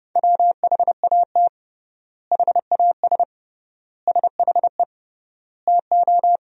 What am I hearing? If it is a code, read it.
Morse code, heard as a keyed tone, about 29 wpm, 700 Hz. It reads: WHAT HAS SHE TO
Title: 3x Word-Spacing - Sentences from Top 100 Words - 30wpm